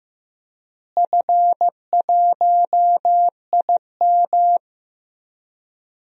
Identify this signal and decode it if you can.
Morse code, heard as a keyed tone, about 15 wpm, 700 Hz. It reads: F1IM
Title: DXCC-100 Call Signs - 4 Characters - Repeated 3x - 15wpm